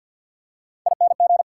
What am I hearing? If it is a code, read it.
Morse code, heard as a keyed tone, about 50 wpm, 690 Hz. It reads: ING
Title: Prefixes and Suffixes 50wpm